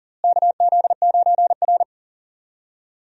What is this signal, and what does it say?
Morse code, heard as a keyed tone, about 37 wpm, 700 Hz. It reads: KZ9R
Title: USA Call Signs - 4 Characters - Repeated 3x - 40wpm